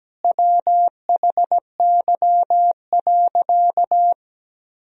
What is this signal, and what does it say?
Morse code, heard as a keyed tone, about 17 wpm, 700 Hz. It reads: WHY.
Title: Sentences from Top 1000 Words - Review Part 3 - 17wpm